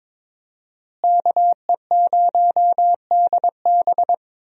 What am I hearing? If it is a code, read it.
Morse code, heard as a keyed tone, about 22 wpm, 700 Hz. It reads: KE0DB
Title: USA Call Signs - 5 Characters - Repeated 3x - 22wpm